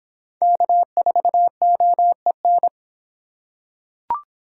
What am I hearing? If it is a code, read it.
Morse code, heard as a keyed tone, about 25 wpm, 700 Hz. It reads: K4OEN